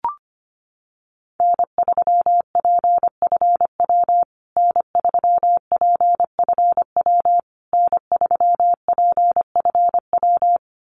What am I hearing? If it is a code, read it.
Morse code, heard as a keyed tone, about 25 wpm, 700 Hz. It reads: N3PFW N3PFW N3PFW
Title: USA Call Signs Repeated 3x 25wpm